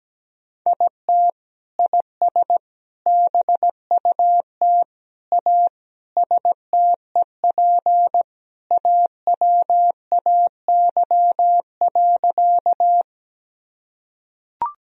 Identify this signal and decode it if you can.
Morse code, heard as a keyed tone, about 17 wpm, 700 Hz. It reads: IT IS BUT A STEP AWAY.
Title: Sentences from Top 1000 Words - Review Part 2 - 17wpm